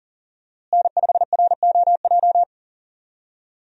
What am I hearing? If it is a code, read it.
Morse code, heard as a keyed tone, about 40 wpm, 700 Hz. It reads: N5ROJ
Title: USA Call Signs - 5 Characters - Repeated 2x - 40wpm